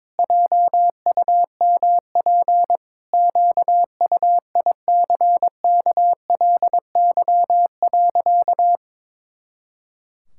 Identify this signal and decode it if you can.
Morse code, heard as a keyed tone, about 22 wpm, 700 Hz. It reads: JUMP QUICKLY.
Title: Pangram Challenge v2 - 22wpm